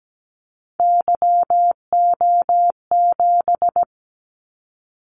Morse code, heard as a keyed tone, about 17 wpm, 700 Hz. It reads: YO7
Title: Callsign Prefixes for Top 25 DXCC - 17wpm